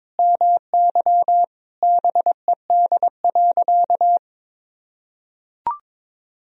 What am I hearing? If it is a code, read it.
Morse code, heard as a keyed tone, about 22 wpm, 700 Hz. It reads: MY BED.
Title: Sentences from Top 1000 Words - Review Part 4 - 22wpm